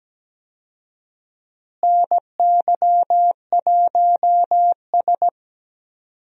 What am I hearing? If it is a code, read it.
Morse code, heard as a keyed tone, about 17 wpm, 700 Hz. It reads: NY1S
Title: USA Call Signs - 4 Characters - Repeated 2x - 17wpm